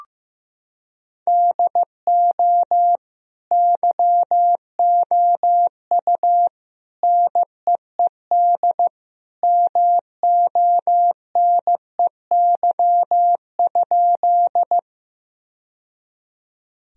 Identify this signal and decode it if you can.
Morse code, heard as a keyed tone, about 16 wpm, 700 Hz. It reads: DO YOU NEED MONEY?
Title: Sentences from Top 1000 Words - Review Part 1 - 15wpm